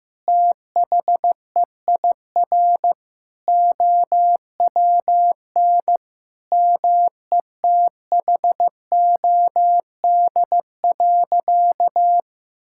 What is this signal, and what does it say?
Morse code, heard as a keyed tone, about 15 wpm, 700 Hz. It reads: THEIR OWN METHOD.
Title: Sentences from Top 1000 Words - Review Part 6 - 15wpm